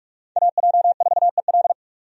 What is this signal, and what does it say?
Morse code, heard as a keyed tone, about 43 wpm, 700 Hz. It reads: AJ4EL
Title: USA Call Signs - 5 Characters - 45wpm